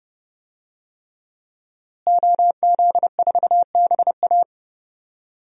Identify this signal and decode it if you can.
Morse code, heard as a keyed tone, about 32 wpm, 700 Hz. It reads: OZ4BA
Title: DXCC-100 Call Signs - 5 Characters - Repeated 3x - 30wpm